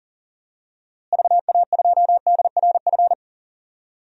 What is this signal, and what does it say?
Morse code, heard as a keyed tone, about 38 wpm, 700 Hz. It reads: VA2DRF
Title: DXCC-100 Call Signs - 6 Characters - Repeated 3x - 40wpm